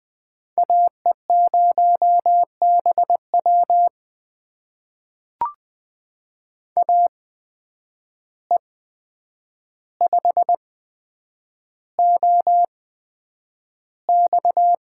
Morse code, heard as a keyed tone, about 20 wpm, spaced as for 9 wpm, 700 Hz. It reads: AE0BW AE5OX
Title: USA Call Signs - 5 Characters - 8x Character Spacing - 20wpm